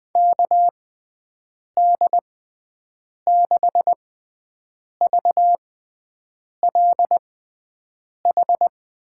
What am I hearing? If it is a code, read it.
Morse code, heard as a keyed tone, about 20 wpm, spaced as for 7 wpm, 700 Hz. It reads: KD6VLH
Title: USA Call Signs - 6 Characters - 6x Character Spacing - 20wpm